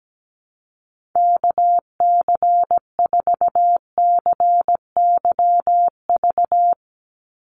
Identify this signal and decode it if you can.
Morse code, heard as a keyed tone, about 17 wpm, 700 Hz. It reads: KC4CYV
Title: USA Call Signs 17wpm